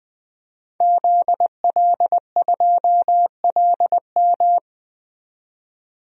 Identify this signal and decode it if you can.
Morse code, heard as a keyed tone, about 20 wpm, 700 Hz. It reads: ZL2LM
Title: DXCC-100 Call Signs - 5 Characters - 20wpm